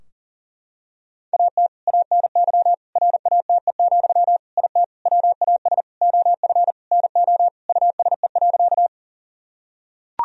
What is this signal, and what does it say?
Morse code, heard as a keyed tone, about 40 wpm, 700 Hz. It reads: AT ANY RATE, IT WAS OF NO USE.
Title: Sentences from Top 1000 Words - Review Part 4 - 40wpm